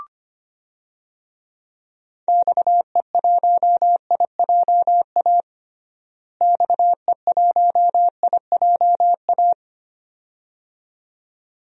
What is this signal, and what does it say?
Morse code, heard as a keyed tone, about 25 wpm, 700 Hz. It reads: XE1IJA XE1IJA
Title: DXCC-100 Call Signs - 6 Characters - Repeated 2x - 25wpm